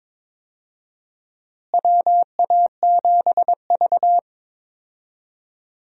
Morse code, heard as a keyed tone, about 22 wpm, 700 Hz. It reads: WA7V
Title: USA Call Signs - 4 Characters - Repeated 2x - 22wpm